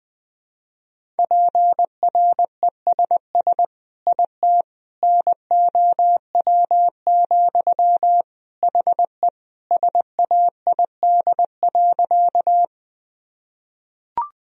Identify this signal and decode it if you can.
Morse code, heard as a keyed tone, about 20 wpm, 700 Hz. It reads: PRESS IT NOW, HE SAID.
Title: Sentences from Top 1000 Words - Review Part 1 - 20wpm